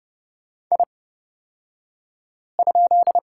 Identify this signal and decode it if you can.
Morse code, heard as a keyed tone, about 29 wpm, spaced as for 5 wpm, 700 Hz. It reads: I?
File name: Lesson 38d - Letter q - Two Characters 30wpm